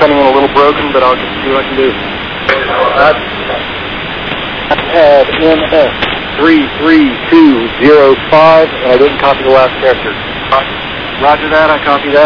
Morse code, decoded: none

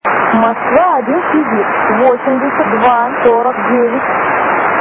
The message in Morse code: none

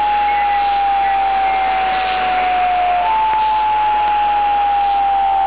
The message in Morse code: none

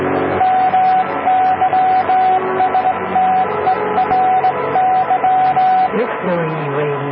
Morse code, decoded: none

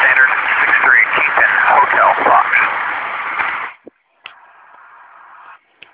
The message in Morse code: none

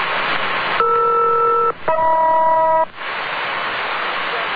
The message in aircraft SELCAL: ER-HM

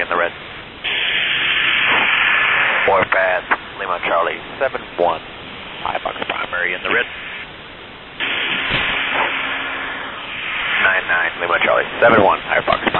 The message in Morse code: none

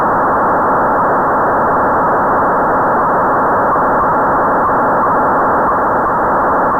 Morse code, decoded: none

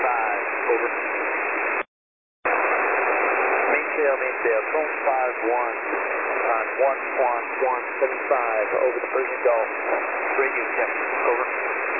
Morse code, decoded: none

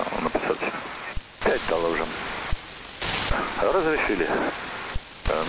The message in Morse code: none

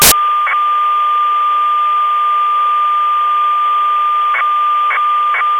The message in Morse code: none

